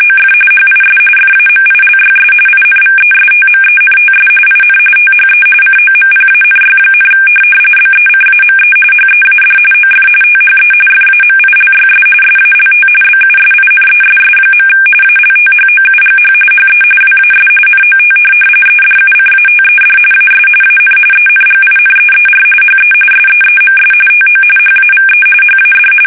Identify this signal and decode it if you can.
RTTY, about 45.45 baud. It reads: OF HIS FAMILYS INVESTMENTS IN THE COMPANY, BECAME HEAD OF THE FAMILYS SALT BUSINESS. ALTHOUGH HE WAS NOT EDUCATED AS AN ENGINEER HE SEEMS TO HAVE HAD
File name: fsk850 / teletype-corp